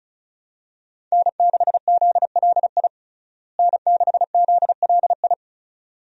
Morse code, heard as a keyed tone, about 36 wpm, 700 Hz. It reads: N6ZLI N6ZLI
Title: USA Call Signs - 5 Characters - Repeated 2x - 35wpm